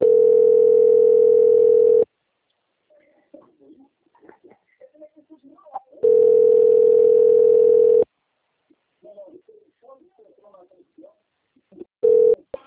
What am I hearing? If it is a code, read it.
Ringback tone (North America), 440 + 480 Hz, with speech.
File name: out- - - - - .14702